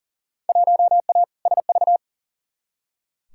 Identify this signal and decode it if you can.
Morse code, heard as a keyed tone, about 40 wpm, 700 Hz. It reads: 1A SV
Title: Field Day Exchanges Encore - 40wpm